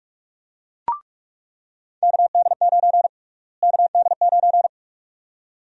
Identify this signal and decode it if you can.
Morse code, heard as a keyed tone, about 46 wpm, 700 Hz. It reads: KD9 KD9
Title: USA Call Sign Prefixes - Repeated 2x - 45wpm